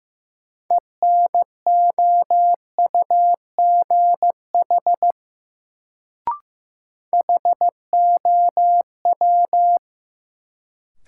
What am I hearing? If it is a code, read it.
Morse code, heard as a keyed tone, about 15 wpm, 700 Hz. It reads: ENOUGH HOW
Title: Sets of 2 Words - New Word Drill - 15wpm